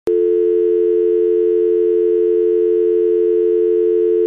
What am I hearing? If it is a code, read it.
Dial tone (North America), 350 + 440 Hz.
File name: videos / TelephoneDialTone 5bf7e66cea2713 / TelephoneDialTone 5bf7e66cea2713